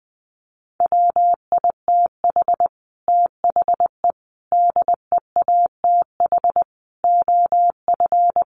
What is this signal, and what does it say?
Morse code, heard as a keyed tone, about 20 wpm, 700 Hz. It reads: WITH THE DEATH OF
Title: A Sound Of Thunder 20wpm